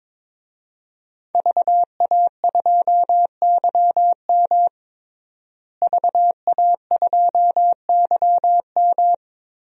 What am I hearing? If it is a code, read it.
Morse code, heard as a keyed tone, about 22 wpm, 700 Hz. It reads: VA2YM VA2YM
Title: DXCC-100 Call Signs - 5 Characters - Repeated 3x - 22wpm